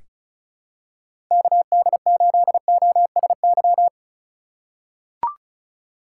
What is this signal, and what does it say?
Morse code, heard as a keyed tone, about 36 wpm, 700 Hz. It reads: KD8OSY
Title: USA Call Signs - 6 Characters - 2x Character Spacing - 35wpm